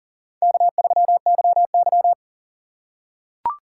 Morse code, heard as a keyed tone, about 40 wpm, 700 Hz. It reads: K3YY
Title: USA Call Signs - 4 Characters - 6x Character Spacing - 40wpm